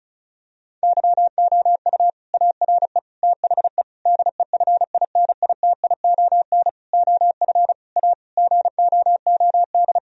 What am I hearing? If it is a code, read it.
Morse code, heard as a keyed tone, about 34 wpm, 700 Hz. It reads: YOU ARE THE DEFINITION OF A GOOD
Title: Sentences from Top 1000 Words - Review Part 4 - 35wpm